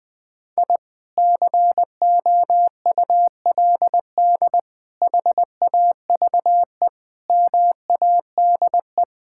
Morse code, heard as a keyed tone, about 20 wpm, 700 Hz. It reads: I COULD HAVE MADE
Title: Sentences from Top 1000 Words - Review Part 5 - 20wpm